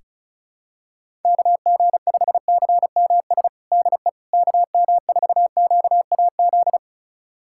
Morse code, heard as a keyed tone, about 34 wpm, 700 Hz. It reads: KG5CMS DE KM4QAZ
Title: USA Callsign DE Callsign - 6 Characters - 35wpm